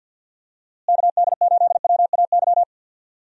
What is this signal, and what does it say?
Morse code, heard as a keyed tone, about 51 wpm, 700 Hz. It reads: KD8WAY
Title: USA Call Signs - 6 Characters - Repeated 2x - 50wpm